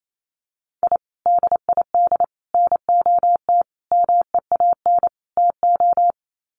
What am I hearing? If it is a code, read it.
Morse code, heard as a keyed tone, about 28 wpm, 700 Hz. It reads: I DID NOT MEAN TO
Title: Sets of 5 Words 28wpm